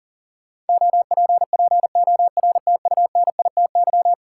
Morse code, heard as a keyed tone, about 40 wpm, 700 Hz. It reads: OPPORTUNITY
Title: Sentences from Top 1000 Words - New Word Drill - 40wpm